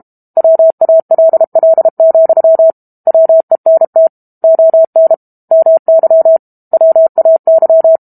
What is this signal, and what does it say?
Morse code, heard as a keyed tone, about 32 wpm, 650 Hz. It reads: WALL, WENT ON MY WAY
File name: Chapter 6 - 32 WPM